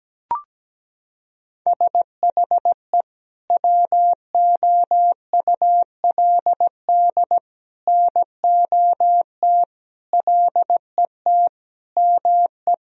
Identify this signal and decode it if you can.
Morse code, heard as a keyed tone, about 17 wpm, 700 Hz. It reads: SHE WOULD NOT LET ME
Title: Sentences from Top 1000 Words - Review Part 1 - 17wpm